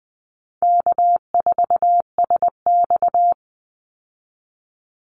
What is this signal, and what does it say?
Morse code, heard as a keyed tone, about 20 wpm, 700 Hz. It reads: K4SX
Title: Call Signs - 20wpm